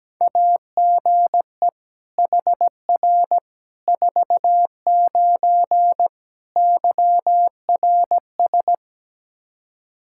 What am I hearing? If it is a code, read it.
Morse code, heard as a keyed tone, about 17 wpm, 700 Hz. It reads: AGE HR 49 YRS
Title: Practice QSO - Exchange 3 - More Info - 17wpm